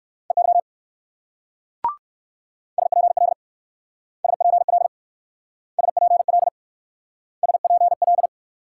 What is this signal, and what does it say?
Morse code, heard as a keyed tone, about 57 wpm, 700 Hz. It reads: EY SPL SPL SPL SPL